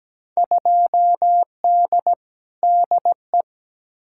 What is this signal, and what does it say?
Morse code, heard as a keyed tone, about 17 wpm, 700 Hz. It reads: 2D DE
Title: Field Day Exchanges Encore - 17wpm